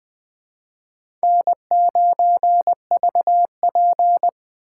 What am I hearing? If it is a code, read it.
Morse code, heard as a keyed tone, about 21 wpm, 700 Hz. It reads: N9VP
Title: USA Call Signs - 4 Characters - 4x Character Spacing - 20wpm